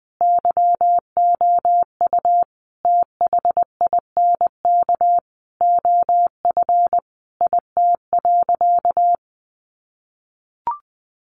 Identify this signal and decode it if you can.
Morse code, heard as a keyed tone, about 20 wpm, 700 Hz. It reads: YOU THINK OF IT.
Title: Sentences from Top 200 Words 20wpm